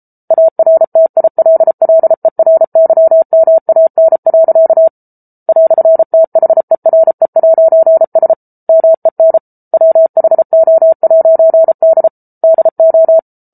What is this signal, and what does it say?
Morse code, heard as a keyed tone, about 34 wpm, 650 Hz. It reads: ARTILLERYMAN. "THERE'S MEN WHO'D DO